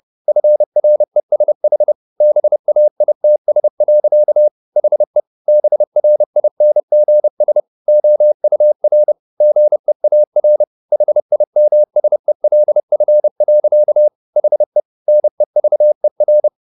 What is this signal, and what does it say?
Morse code, heard as a keyed tone, about 30 wpm, 600 Hz. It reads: FRESH BAITS. HE BRINGS OUR GEAR HIMSELF. HE NEVER
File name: the old man and the sea / the old man and the sea 1 30wpm